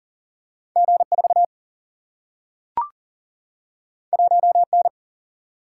Morse code, heard as a keyed tone, about 38 wpm, 700 Hz. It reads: G 4 1 N
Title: Single-Letter Single-Number 40wpm